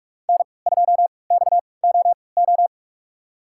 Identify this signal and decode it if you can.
Morse code, heard as a keyed tone, about 41 wpm, spaced as for 25 wpm, 700 Hz. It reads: N2XOO
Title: USA Call Signs - 5 Characters - 3x Character Spacing - 45wpm